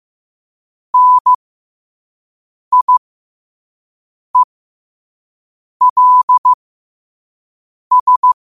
Morse code, heard as a keyed tone, about 15 wpm, spaced as for 5 wpm, 1000 Hz. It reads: NIELS
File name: qotdfiles / 5wpm